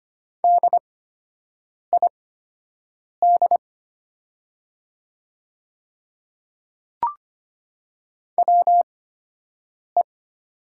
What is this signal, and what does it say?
Morse code, heard as a keyed tone, about 25 wpm, spaced as for 7 wpm, 690 Hz. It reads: DID WE
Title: 8x Character Spacing - Top 300 Words - 25wpm